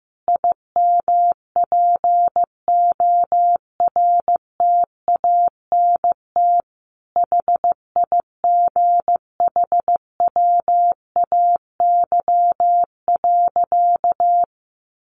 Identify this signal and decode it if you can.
Morse code, heard as a keyed tone, about 15 wpm, 700 Hz. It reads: IMPORTANT HIGHWAY.
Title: Sentences from Top 1400 Words 15wpm